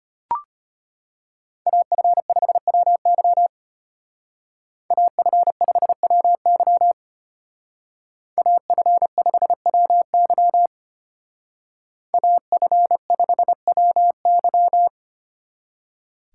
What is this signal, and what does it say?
Morse code, heard as a keyed tone, about 30 wpm, 700 Hz. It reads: AF5WY AF5WY AF5WY AF5WY
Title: Speed Racing - USA 2x2 Call Signs 25wpm